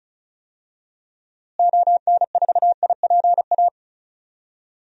Morse code, heard as a keyed tone, about 36 wpm, 700 Hz. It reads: ON4IPA
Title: DXCC-100 Call Signs - 6 Characters - Repeated 3x - 35wpm